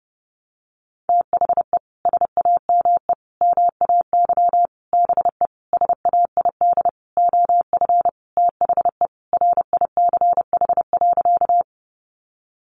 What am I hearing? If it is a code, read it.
Morse code, heard as a keyed tone, about 30 wpm, 700 Hz. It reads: THE SAME MAY BE SAID OF THE RICH.
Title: Sentences from Top 1100 Words 30wpm